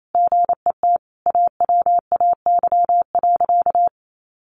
Morse code, heard as a keyed tone, about 28 wpm, 700 Hz. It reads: GET AWAY.
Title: Sentences from Top 600 Words 28wpm